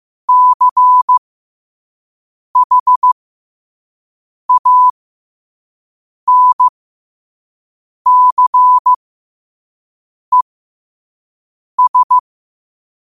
Morse code, heard as a keyed tone, about 15 wpm, spaced as for 5 wpm, 1000 Hz. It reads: CHANCES
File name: qotdfiles / 5wpm